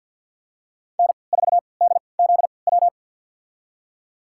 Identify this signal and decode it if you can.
Morse code, heard as a keyed tone, about 47 wpm, spaced as for 28 wpm, 700 Hz. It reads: N4DZW
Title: USA Call Signs - 3x Character Spacing - 50wpm